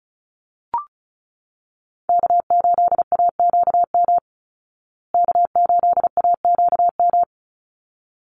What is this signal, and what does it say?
Morse code, heard as a keyed tone, about 34 wpm, 700 Hz. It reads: K8AQM K8AQM
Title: Call Signs Repeated 3x - 35wpm